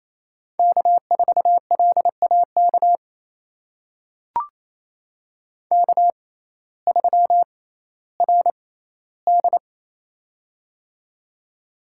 Morse code, heard as a keyed tone, about 28 wpm, spaced as for 15 wpm, 700 Hz. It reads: K4LAK K3RD
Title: USA Call Signs - 6x Character Spacing - 28wpm